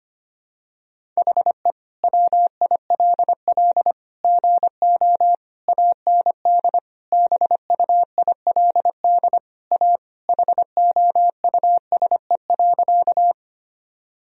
Morse code, heard as a keyed tone, about 25 wpm, 700 Hz. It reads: HE WILL GO AND BUILD A HOUSE.
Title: Sentences from Top 1000 Words - Review Part 1 - 25wpm